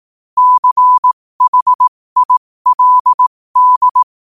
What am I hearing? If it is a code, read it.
Morse code, heard as a keyed tone, about 18 wpm, spaced as for 15 wpm, 1000 Hz. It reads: CHILD
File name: qotdfiles / 15wpm